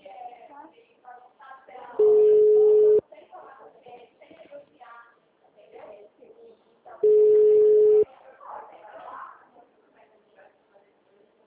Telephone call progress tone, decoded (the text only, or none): ringback tone